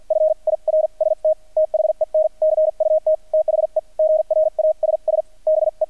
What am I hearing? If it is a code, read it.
Morse code, heard as a keyed tone, about 48 wpm, spaced as for 41 wpm, 630 Hz. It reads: FIRST THEN CUT THE GRASS BE
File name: archived / CRSnet